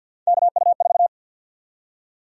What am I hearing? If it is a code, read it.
Morse code, heard as a keyed tone, about 47 wpm, 700 Hz. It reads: KU4
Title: USA Call Sign Prefixes - 50wpm